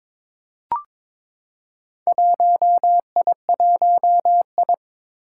Morse code, heard as a keyed tone, about 23 wpm, 700 Hz. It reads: 1I1I
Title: Four Challenging Letter-Number Combinations 22wpm